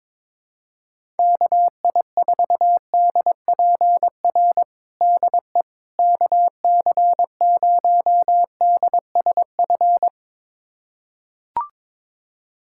Morse code, heard as a keyed tone, about 21 wpm, 700 Hz. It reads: KI4DPR DE KC0DSF
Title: USA Callsign DE Callsign 22wpm